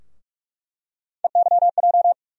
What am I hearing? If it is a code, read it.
Morse code, heard as a keyed tone, about 46 wpm, 700 Hz. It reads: EYJ
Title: USA Call Sign Postfixes - 3 Letters 45wpm